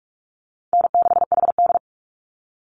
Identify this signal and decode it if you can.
Morse code, heard as a keyed tone, about 44 wpm, 700 Hz. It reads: N6HD
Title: Easy Call Signs Repeated 2x - 45wpm